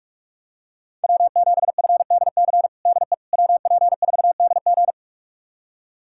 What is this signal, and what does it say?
Morse code, heard as a keyed tone, about 47 wpm, 700 Hz. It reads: W7FDC DE WP4DG
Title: USA Callsign DE Callsign - 5 Characters - 45wpm